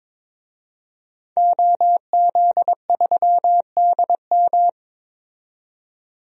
Morse code, heard as a keyed tone, about 21 wpm, 700 Hz. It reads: OZ3DM
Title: DXCC-100 Call Signs - 5 Characters - 22wpm